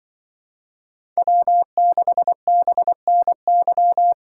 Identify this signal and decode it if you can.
Morse code, heard as a keyed tone, about 24 wpm, 700 Hz. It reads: W6BNY
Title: Speed Racing - USA Call Signs 20wpm